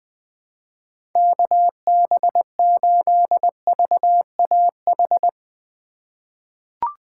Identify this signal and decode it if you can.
Morse code, heard as a keyed tone, about 21 wpm, 700 Hz. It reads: KB8VAH
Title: USA Call Signs - 3x Character Spacing - 20wpm